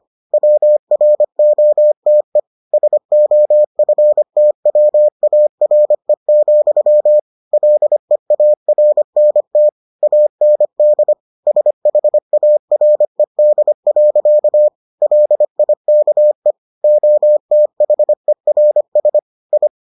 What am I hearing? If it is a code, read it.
Morse code, heard as a keyed tone, about 25 wpm, 600 Hz. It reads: WROTE SOFTWARE, LEARNT AND SHARED. LIKE OTHERS I